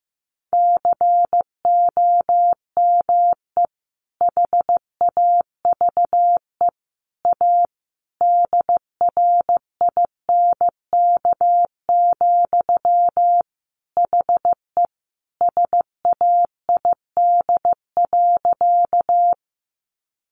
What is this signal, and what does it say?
Morse code, heard as a keyed tone, about 15 wpm, 700 Hz. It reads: COME HAVE A DRINK, HE SAID.
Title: Sentences from Top 1500 Words 15wpm